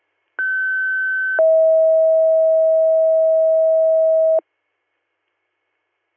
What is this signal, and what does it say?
Two-tone page, heard as a one-second tone, then a three-second tone, about 1550 then 640 Hz. It is heard from 0.4 to 4.4 s.